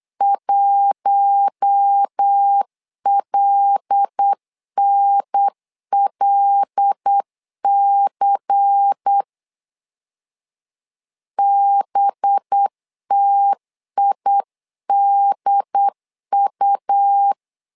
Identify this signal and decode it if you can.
Morse code, heard as a keyed tone, about 8 wpm, 800 Hz. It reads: 1LNLC BTIDU
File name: LEZIONI / 5 LBVC / 5 LBVC 8wpm 2935